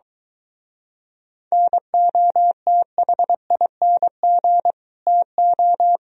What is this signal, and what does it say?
Morse code, heard as a keyed tone, about 23 wpm, 700 Hz. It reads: NOTHING TO